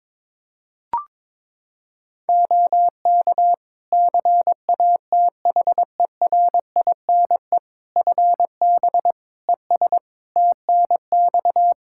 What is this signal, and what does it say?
Morse code, heard as a keyed tone, about 22 wpm, 700 Hz. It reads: OK CATHERINE FB ES TNX
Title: Practice QSO - Exchange 2 - Info - 22wpm